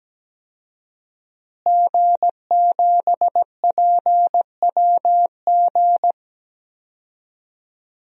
Morse code, heard as a keyed tone, about 17 wpm, 700 Hz. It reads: G7PWG
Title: DXCC-100 Call Signs - 5 Characters - Repeated 3x - 17wpm